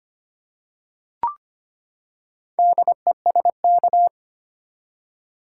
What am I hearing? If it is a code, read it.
Morse code, heard as a keyed tone, about 25 wpm, 700 Hz. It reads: DESK
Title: Four Letter Words - 25wpm